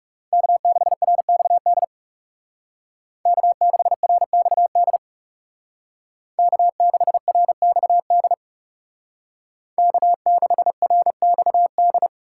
Morse code, heard as a keyed tone, about 36 wpm, 700 Hz. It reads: K6RXD K6RXD K6RXD K6RXD
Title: Speed Racing - USA 1x3 Call Signs 30wpm